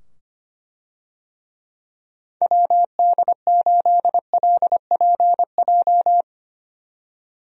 Morse code, heard as a keyed tone, about 25 wpm, 700 Hz. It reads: WD8LPJ
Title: USA Call Signs - 6 Characters - 25wpm